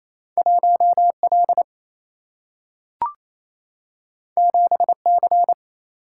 Morse code, heard as a keyed tone, about 28 wpm, 700 Hz. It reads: 1L 7C
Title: Single-Letter Single-Number 28wpm